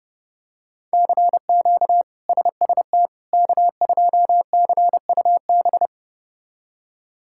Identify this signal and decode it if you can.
Morse code, heard as a keyed tone, about 30 wpm, 700 Hz. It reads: CQ SST K2CUB
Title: K1USN SST QSO Practice - 30wpm